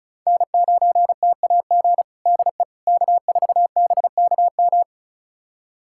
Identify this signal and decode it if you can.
Morse code, heard as a keyed tone, about 34 wpm, 690 Hz. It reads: N9TAG DE K4BKM
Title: USA Callsign DE Callsign - 5 Characters - 35wpm